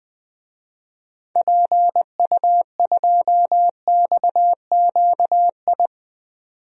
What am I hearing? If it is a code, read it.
Morse code, heard as a keyed tone, about 20 wpm, 700 Hz. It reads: PU2XQI
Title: DXCC-100 Call Signs - 6 Characters - 20wpm